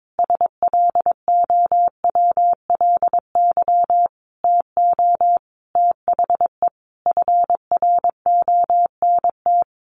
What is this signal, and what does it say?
Morse code, heard as a keyed tone, about 22 wpm, 700 Hz. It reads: SLOWLY TO THE FRONT
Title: Sentences from Top 1400 Words 22wpm